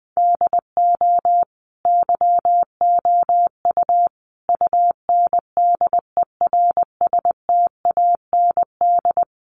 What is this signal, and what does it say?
Morse code, heard as a keyed tone, about 20 wpm, 700 Hz. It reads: DO YOU UNDERSTAND
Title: Sentences from Top 600 Words 20wpm